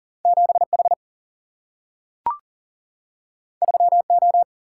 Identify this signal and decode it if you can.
Morse code, heard as a keyed tone, about 40 wpm, 700 Hz. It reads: 7H 3O
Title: Single-Letter Single-Number 40wpm